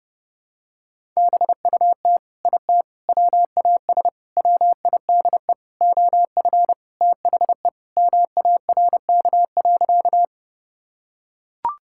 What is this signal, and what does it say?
Morse code, heard as a keyed tone, about 30 wpm, 700 Hz. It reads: BUT IT WAS WIDE OF THE MARK.
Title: Sentences from Top 1000 Words - Review Part 6 - 30wpm